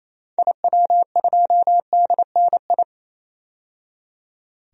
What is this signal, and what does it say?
Morse code, heard as a keyed tone, about 28 wpm, 700 Hz. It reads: IW2DNI
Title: DXCC-100 Call Signs - 6 Characters - 28wpm